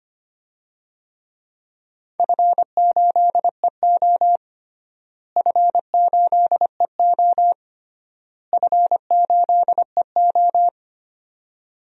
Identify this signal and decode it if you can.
Morse code, heard as a keyed tone, about 25 wpm, 700 Hz. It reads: F8EO F8EO F8EO
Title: DXCC-100 Call Signs - 4 Characters - Repeated 3x - 25wpm